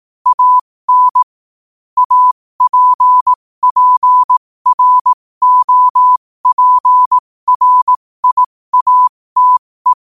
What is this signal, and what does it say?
Morse code, heard as a keyed tone, about 18 wpm, spaced as for 15 wpm, 1000 Hz. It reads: AN APPROPRIATE
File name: qotdfiles / 15wpm